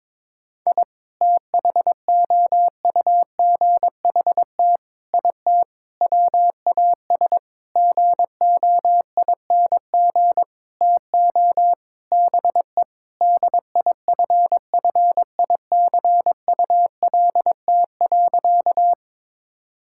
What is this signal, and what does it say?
Morse code, heard as a keyed tone, about 22 wpm, 700 Hz. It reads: I THOUGHT IT WAS GOING TO BE DIFFICULT.
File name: Sentences with 7-Word Phrases - 22wpm